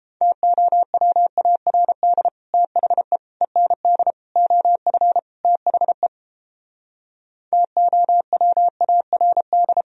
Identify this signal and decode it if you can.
Morse code, heard as a keyed tone, about 32 wpm, 700 Hz. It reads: TOWARD THE END OF THE TOWARD